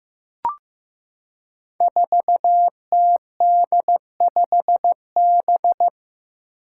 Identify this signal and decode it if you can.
Morse code, heard as a keyed tone, about 15 wpm, 700 Hz. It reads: 4TD5B
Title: Five Challenging Letter-Number Combinations 15wpm